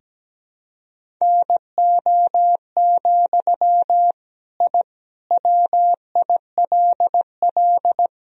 Morse code, heard as a keyed tone, about 17 wpm, 700 Hz. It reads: NO, I WILL
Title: Sentences from Top 1000 Words - Review Part 1 - 17wpm